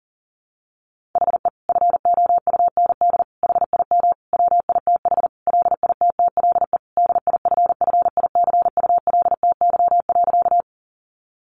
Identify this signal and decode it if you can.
Morse code, heard as a keyed tone, about 37 wpm, 700 Hz. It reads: HE FOUND HIM WITH LITTLE DIFFICULTY.
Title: Sentences from Top 1500 Words 40wpm